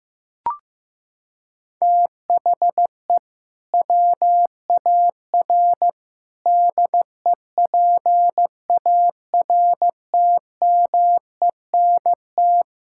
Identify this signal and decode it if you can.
Morse code, heard as a keyed tone, about 15 wpm, 700 Hz. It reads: THE WAR DEPARTMENT
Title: Sentences from Top 1000 Words - Review Part 5 - 15wpm